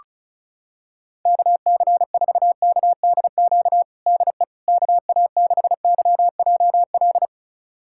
Morse code, heard as a keyed tone, about 36 wpm, 700 Hz. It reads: KC4KDQ DE KA6YJL
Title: USA Callsign DE Callsign - 6 Characters - 35wpm